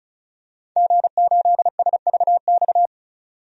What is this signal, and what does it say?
Morse code, heard as a keyed tone, about 36 wpm, 700 Hz. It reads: G8SVX
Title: DXCC-100 Call Signs - 5 Characters - 35wpm